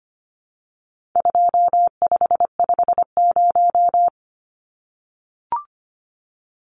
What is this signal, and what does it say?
Morse code, heard as a keyed tone, about 25 wpm, 700 Hz. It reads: 2550
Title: Numbers - 4-Digit 25wpm